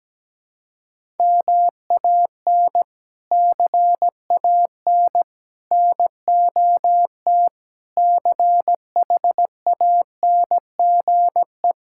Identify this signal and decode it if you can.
Morse code, heard as a keyed tone, about 17 wpm, 700 Hz. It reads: MAN CAN NOT CHANGE
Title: Sentences from Top 1000 Words - Review Part 5 - 17wpm